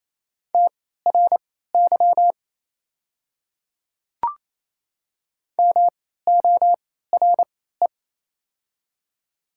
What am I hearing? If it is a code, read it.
Morse code, heard as a keyed tone, about 27 wpm, spaced as for 16 wpm, 700 Hz. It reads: TRY MORE
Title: 3x Character Spacing - Top 300 Words - 28wpm